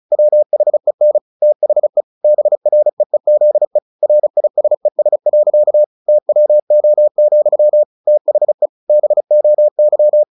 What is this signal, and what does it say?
Morse code, heard as a keyed tone, about 34 wpm, 600 Hz. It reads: WHEN THE BREEZE RISES. TWO, THE BOY
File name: the old man and the sea / the old man and the sea 1 35wpm